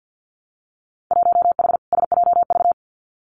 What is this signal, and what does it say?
Morse code, heard as a keyed tone, about 50 wpm, 700 Hz. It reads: 1H SJV
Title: Winter Field Day Exchanges 50wpm